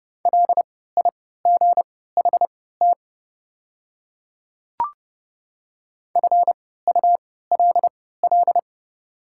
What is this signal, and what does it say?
Morse code, heard as a keyed tone, about 30 wpm, spaced as for 17 wpm, 690 Hz. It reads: LIGHT FULL
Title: 3x Character Spacing - Top 500 Words - 30wpm